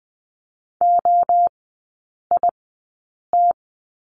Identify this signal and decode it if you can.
Morse code, heard as a keyed tone, about 20 wpm, spaced as for 8 wpm, 700 Hz. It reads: OIT
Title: 067 Lesson 12e - Letter d - Three Characters 20wpm